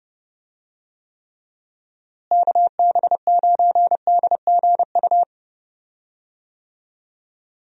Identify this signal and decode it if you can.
Morse code, heard as a keyed tone, about 31 wpm, 700 Hz. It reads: KB9DGU
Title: USA Call Signs - 6 Characters - Repeated 3x - 30wpm